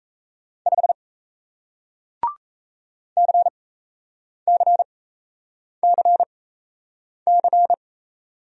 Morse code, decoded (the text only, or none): 5 CCCC